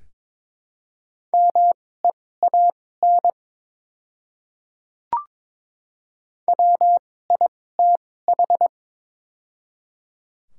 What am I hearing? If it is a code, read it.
Morse code, heard as a keyed tone, about 22 wpm, spaced as for 16 wpm, 700 Hz. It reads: MEAN WITH
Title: 2x Character Spacing - Top 300 Words - 22wpm